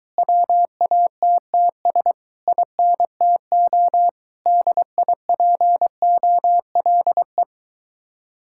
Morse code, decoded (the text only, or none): WATTS INTO DIPOLE